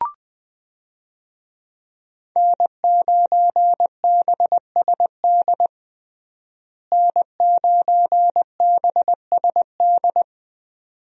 Morse code, heard as a keyed tone, about 20 wpm, 700 Hz. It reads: N9BSD N9BSD
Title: USA Call Signs - 5 Characters - Repeated 2x - 20wpm